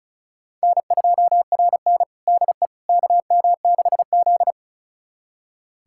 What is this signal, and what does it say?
Morse code, heard as a keyed tone, about 36 wpm, 690 Hz. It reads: N2RN DE KM6Z
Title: USA Callsign DE Callsign - 4 Characters - 35wpm